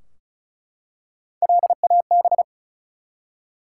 Morse code, heard as a keyed tone, about 36 wpm, 700 Hz. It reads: LAB
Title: Three Letter Words - 35wpm